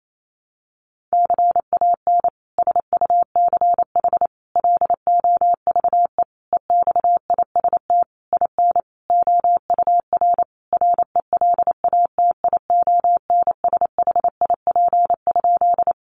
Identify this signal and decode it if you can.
Morse code, heard as a keyed tone, about 28 wpm, 700 Hz. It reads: CAN SUCH LOVE EXIST IN OUR RELATIONSHIP?
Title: Sentences from Top 1100 Words 28wpm